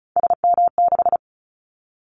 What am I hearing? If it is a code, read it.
Morse code, heard as a keyed tone, about 36 wpm, 700 Hz. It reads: SM6
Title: Callsign Prefixes for Top 25 DXCC - 35wpm